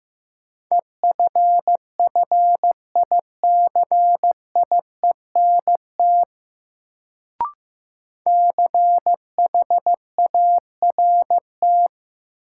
Morse code, heard as a keyed tone, about 15 wpm, 700 Hz. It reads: EFFICIENT CHART
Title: Sentences from Top 1300 Words - New Word Drill - 15wpm